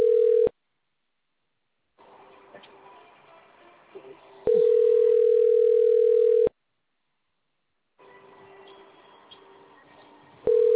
Ringback tone (North America), 440 + 480 Hz.